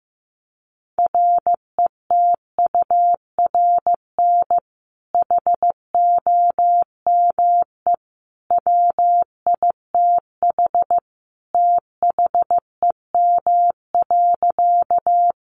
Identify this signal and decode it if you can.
Morse code, heard as a keyed tone, about 14 wpm, 700 Hz. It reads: RETURN HOME WITH THEM.
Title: Sentences from Top 800 Words 15wpm